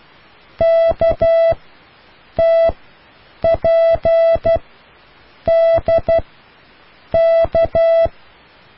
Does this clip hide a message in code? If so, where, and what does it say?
Yes: Morse code from 0.6 to 8.1 s, heard as a keyed tone, about 12 wpm, spaced as for 7 wpm, 660 Hz. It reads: KTPDK